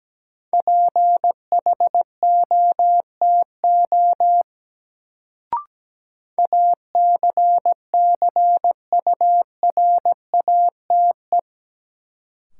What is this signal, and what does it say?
Morse code, heard as a keyed tone, about 17 wpm, 700 Hz. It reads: PHOTO ACCURATE
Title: Sentences from Top 1200 Words - New Word Drill Alt - 17wpm